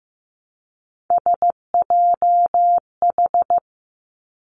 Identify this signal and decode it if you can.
Morse code, heard as a keyed tone, about 15 wpm, 700 Hz. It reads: SJH
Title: Three Challenging Letter-Number Combinations 15wpm